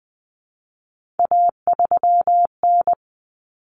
Morse code, heard as a keyed tone, about 20 wpm, 700 Hz. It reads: A3N
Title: Common Cut Numbers - 3-Digit 20wpm